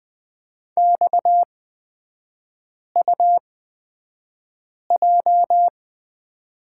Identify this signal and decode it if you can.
Morse code, heard as a keyed tone, about 20 wpm, spaced as for 5 wpm, 700 Hz. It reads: XUJ
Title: 255 Lesson 41e - Period - Three Characters 20wpm